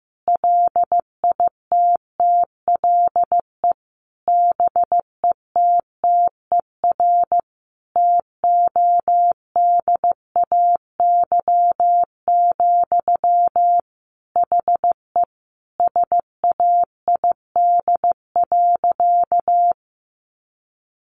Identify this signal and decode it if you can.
Morse code, heard as a keyed tone, about 15 wpm, 700 Hz. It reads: LITTLE BETTER TODAY, HE SAID.
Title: Sentences from Top 600 Words 15wpm